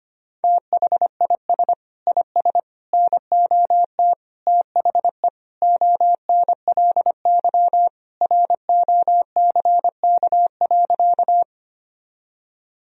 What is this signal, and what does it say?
Morse code, heard as a keyed tone, about 25 wpm, 700 Hz. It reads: THIS IS NOT THE ONLY ROCK.
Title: Sentences from Top 1000 Words - Review Part 2 - 25wpm